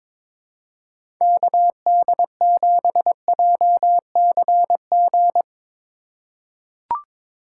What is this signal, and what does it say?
Morse code, heard as a keyed tone, about 22 wpm, 700 Hz. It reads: KD7JCG
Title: USA Call Signs - 6 Characters - 4x Character Spacing - 22wpm